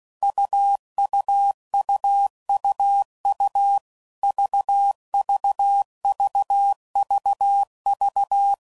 Morse code, heard as a keyed tone, about 16 wpm, 800 Hz. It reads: UUUUU VVVVV
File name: Combinations / Difficult Combos / AUV AUV4